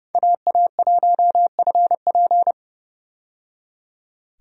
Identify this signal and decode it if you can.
Morse code, heard as a keyed tone, about 30 wpm, 700 Hz. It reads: AA1FP
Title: USA Call Signs - 5 Characters - 30wpm